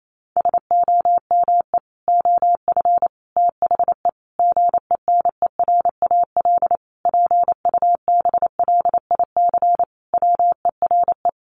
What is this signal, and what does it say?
Morse code, heard as a keyed tone, about 28 wpm, 700 Hz. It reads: SOME OF THE GENERAL PUBLIC WERE
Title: Sentences from Top 800 Words 28wpm